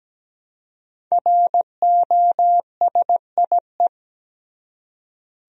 Morse code, heard as a keyed tone, about 17 wpm, 700 Hz. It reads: ROSIE
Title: USA Common Names - Max 5 Letters - 17wpm